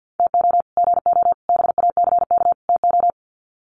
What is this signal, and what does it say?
Morse code, heard as a keyed tone, about 50 wpm, 700 Hz. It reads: TO GO BACK TO
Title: Sets of 4 Words 50wpm